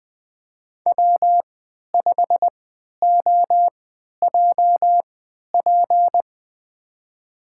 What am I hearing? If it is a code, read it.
Morse code, heard as a keyed tone, about 20 wpm, spaced as for 11 wpm, 700 Hz. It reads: W5OJP
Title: USA Call Signs - 5 Characters - 3x Character Spacing - 20wpm